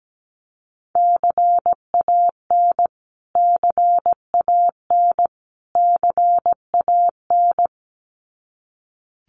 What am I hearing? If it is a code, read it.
Morse code, heard as a keyed tone, about 17 wpm, 700 Hz. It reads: CAN CAN CAN
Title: Top 500 Words Repeated 17wpm